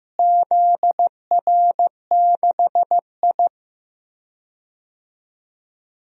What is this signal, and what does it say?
Morse code, heard as a keyed tone, about 15 wpm, 700 Hz. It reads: ZR6I
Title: DXCC-100 Call Signs - 4 Characters - Repeated 3x - 15wpm